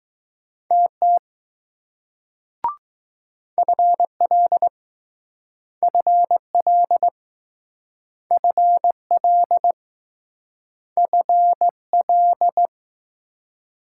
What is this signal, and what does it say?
Morse code, heard as a keyed tone, about 20 wpm, 700 Hz. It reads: TT FL FL FL FL